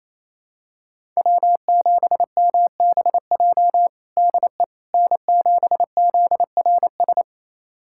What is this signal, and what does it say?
Morse code, heard as a keyed tone, about 28 wpm, 700 Hz. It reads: W7MBJ DE N7ZRS
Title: USA Callsign DE Callsign - 5 Characters - 28wpm